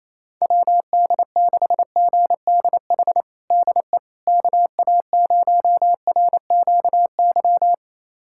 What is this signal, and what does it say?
Morse code, heard as a keyed tone, about 28 wpm, 700 Hz. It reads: WD6GDH DE KA0RQY
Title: USA Callsign DE Callsign - 6 Characters - 28wpm